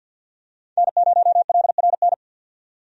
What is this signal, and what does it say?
Morse code, heard as a keyed tone, about 48 wpm, 700 Hz. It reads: N0LRN
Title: USA Call Signs - 5 Characters - 50wpm